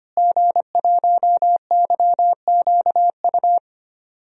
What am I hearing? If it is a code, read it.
Morse code, heard as a keyed tone, about 25 wpm, 700 Hz. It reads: G1YQU
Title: DXCC-100 Call Signs - 5 Characters - Repeated 2x - 25wpm